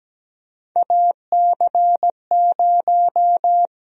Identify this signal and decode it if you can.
Morse code, heard as a keyed tone, about 17 wpm, 700 Hz. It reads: AC0
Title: USA Call Sign Prefixes - 17wpm